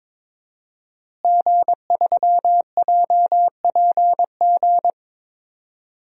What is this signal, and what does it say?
Morse code, heard as a keyed tone, about 21 wpm, 700 Hz. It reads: G3JPG
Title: DXCC-100 Call Signs - 5 Characters - Repeated 2x - 22wpm